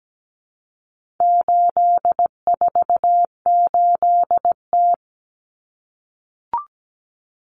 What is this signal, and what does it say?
Morse code, heard as a keyed tone, about 17 wpm, 700 Hz. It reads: 848T
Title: Common Cut Numbers - 4-Digit 17wpm